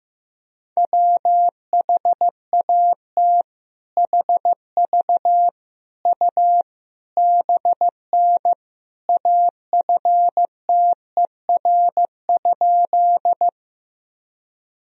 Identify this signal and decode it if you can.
Morse code, heard as a keyed tone, about 14 wpm, 700 Hz. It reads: WHAT HV U BN AFTER?
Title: Sentences from Top 200 Words and Abbreviations 15wpm